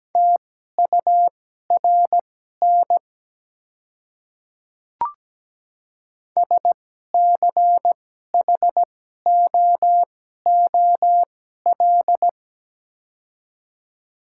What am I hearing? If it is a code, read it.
Morse code, heard as a keyed tone, about 17 wpm, spaced as for 12 wpm, 700 Hz. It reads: TURN SCHOOL
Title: 2x Character Spacing - Top 300 Words - 17wpm